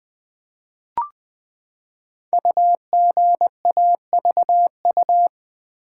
Morse code, heard as a keyed tone, about 20 wpm, 700 Hz. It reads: UGAVU
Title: Code Groups - 5 Letter:Numbers - 20wpm